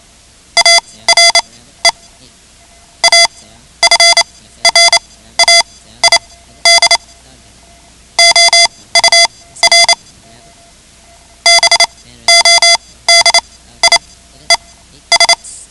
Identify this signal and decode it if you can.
Morse code, heard as a keyed tone, about 28 wpm, spaced as for 15 wpm, 2050 Hz. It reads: ARE AFRAID OUR BODIES
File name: ham radio / CodeQuick / Code Quick 10